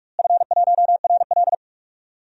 Morse code, heard as a keyed tone, about 41 wpm, 700 Hz. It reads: F1RL
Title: DXCC-100 Call Signs - 4 Characters - Repeated 2x - 45wpm